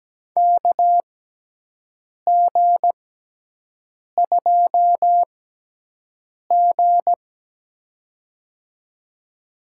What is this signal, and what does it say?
Morse code, heard as a keyed tone, about 17 wpm, spaced as for 6 wpm, 700 Hz. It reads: KG2G
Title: USA Call Signs - 4 Characters - 6x Character Spacing - 17wpm